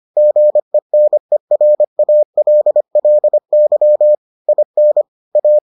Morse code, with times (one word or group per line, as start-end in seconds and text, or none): 0.2-4.1 GENERALLY
4.5-5.0 IN
5.4-5.6 A